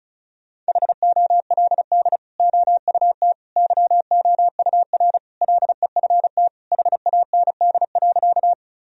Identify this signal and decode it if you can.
Morse code, heard as a keyed tone, about 34 wpm, 700 Hz. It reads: HOLD OUT YOUR LEFT HAND.
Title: Sentences from Top 1000 Words - Review Part 2 - 35wpm